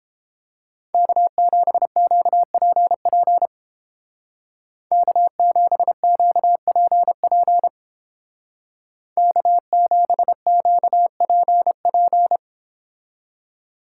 Morse code, heard as a keyed tone, about 31 wpm, 700 Hz. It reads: K7QPP K7QPP K7QPP